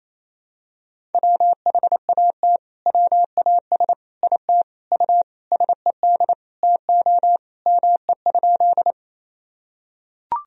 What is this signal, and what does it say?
Morse code, heard as a keyed tone, about 28 wpm, 700 Hz. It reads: WHAT WAS IT U SED TO ME?
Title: Sentences from Top 200 Words and Abbreviations 28wpm